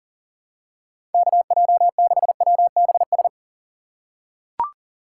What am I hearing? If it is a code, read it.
Morse code, heard as a keyed tone, about 40 wpm, 700 Hz. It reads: KJ6WBS
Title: USA Call Signs - 6 Characters - 2x Character Spacing - 40wpm